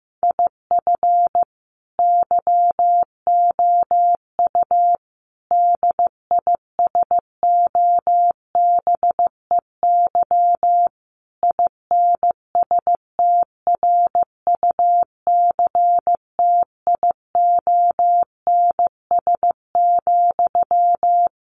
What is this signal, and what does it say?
Morse code, heard as a keyed tone, about 15 wpm, 700 Hz. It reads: IF YOU DISOBEY INSTRUCTIONS,
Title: A Sound Of Thunder 15wpm